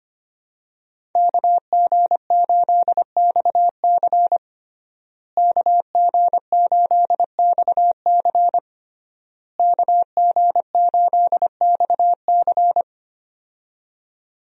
Morse code, heard as a keyed tone, about 25 wpm, 700 Hz. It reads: KG8XC KG8XC KG8XC
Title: USA Call Signs - 5 Characters - Repeated 3x - 25wpm